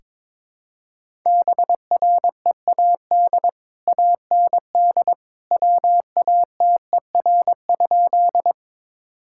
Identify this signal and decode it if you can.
Morse code, heard as a keyed tone, about 21 wpm, 700 Hz. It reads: BREAD AND WATER?
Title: Sentences from Top 1600 Words 22wpm